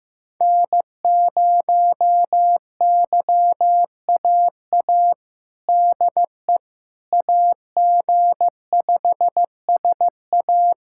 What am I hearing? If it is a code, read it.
Morse code, heard as a keyed tone, about 16 wpm, 700 Hz. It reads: N0YAA DE AG5SA
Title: USA Callsign DE Callsign 15wpm